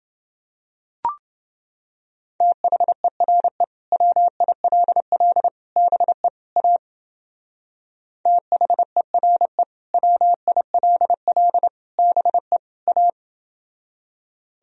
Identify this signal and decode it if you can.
Morse code, heard as a keyed tone, about 29 wpm, 700 Hz. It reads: THERE WILL BE A THERE WILL BE A